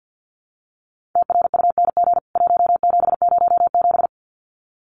Morse code, heard as a keyed tone, about 48 wpm, 700 Hz. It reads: TRUNG 1707
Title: CWT Exchange 50wpm